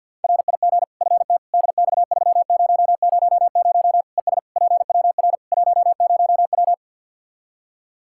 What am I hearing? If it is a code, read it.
Morse code, heard as a keyed tone, about 49 wpm, 700 Hz. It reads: RIG FTDX3000 ES PWR 10W
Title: Practice QSO - Exchange 2 - Info - 50wpm